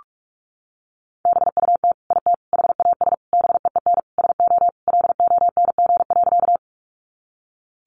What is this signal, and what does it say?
Morse code, heard as a keyed tone, about 44 wpm, 700 Hz. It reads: BUT IT HAS BEEN SO LONG.
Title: Sentences from Top 100 Words 45wpm